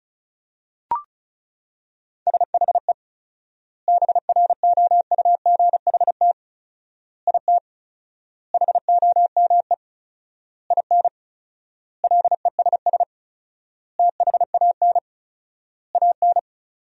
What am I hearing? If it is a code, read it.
Morse code, heard as a keyed tone, about 34 wpm, 700 Hz. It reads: SHE BROUGHT IT HOME IN LESS THAN AN
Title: 4x Word-Spacing - Sentences from Top 500 Words - 35wpm